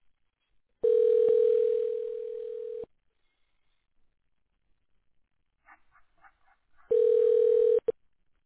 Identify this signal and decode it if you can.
Ringback tone (North America), 440 + 480 Hz.